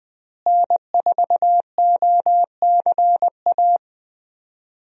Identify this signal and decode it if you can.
Morse code, heard as a keyed tone, about 20 wpm, 700 Hz. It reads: N4OCA
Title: USA Call Signs - 5 Characters - Repeated 3x - 20wpm